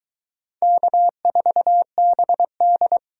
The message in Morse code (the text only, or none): K4BD